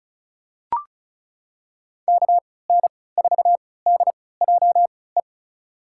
Morse code, heard as a keyed tone, about 34 wpm, spaced as for 20 wpm, 700 Hz. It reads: KN4DJE
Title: USA Call Signs - 6 Characters - 3x Character Spacing - 35wpm